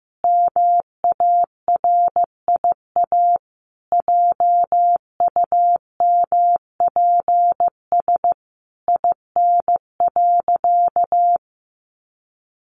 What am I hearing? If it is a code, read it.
Morse code, heard as a keyed tone, about 15 wpm, 700 Hz. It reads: MARIA JUMPS IN.
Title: Easy Phrases 15wpm